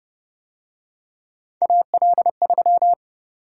Morse code, heard as a keyed tone, about 29 wpm, 700 Hz. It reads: AL3